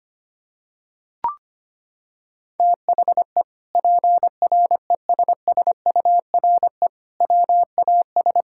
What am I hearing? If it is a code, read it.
Morse code, heard as a keyed tone, about 25 wpm, 700 Hz. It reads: THE PRESSURE WAS
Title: Sentences from Top 1000 Words - Review Part 6 - 25wpm